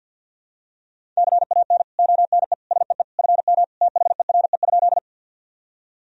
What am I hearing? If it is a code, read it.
Morse code, heard as a keyed tone, about 50 wpm, 700 Hz. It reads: CAN ONE SEE FM THERE?
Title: Sentences from Top 100 Words and Abbreviations 50wpm